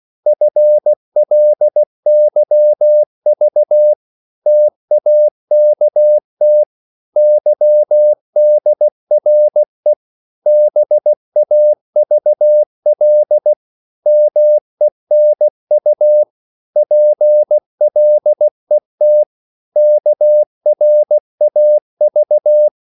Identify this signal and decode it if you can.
Morse code, heard as a keyed tone, about 16 wpm, 600 Hz. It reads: FLYV TAKT YDRE BAVL MENU PLET KRAV
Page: Korte danske ord 16wpm